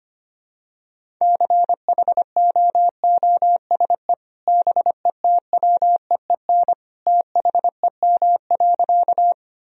Morse code, heard as a keyed tone, about 25 wpm, 700 Hz. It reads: CHOOSE BETWEEN THEM.
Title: Sentences from Top 1000 Words - Review Part 4 - 25wpm